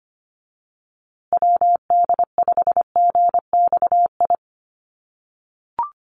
Morse code, heard as a keyed tone, about 25 wpm, 700 Hz. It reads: WD5GXI
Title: USA Call Signs Repeated 2x 25wpm